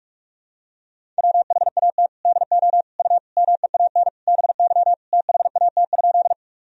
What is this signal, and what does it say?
Morse code, heard as a keyed tone, about 46 wpm, 700 Hz. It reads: WHAT DO U MEAN BY THAT?
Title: Sentences from Top 200 Words and Abbreviations 45wpm